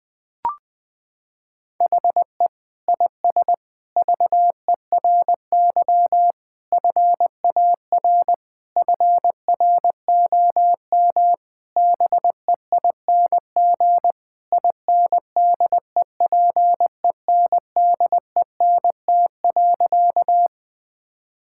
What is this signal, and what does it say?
Morse code, heard as a keyed tone, about 20 wpm, 700 Hz. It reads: HE IS VERY FAR FROM BEING INDEPENDENT.
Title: Sentences from Top 1000 Words - Review Part 6 - 20wpm